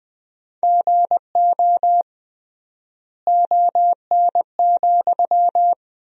Morse code, heard as a keyed tone, about 21 wpm, 700 Hz. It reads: GO ON,
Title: 3x Word-Spacing - Sentences from Top 100 Words - 20wpm